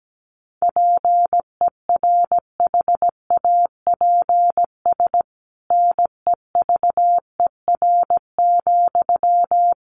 Morse code, heard as a keyed tone, about 17 wpm, 700 Hz. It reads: PERHAPS NEVER,
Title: Sentences from Top 1200 Words 17wpm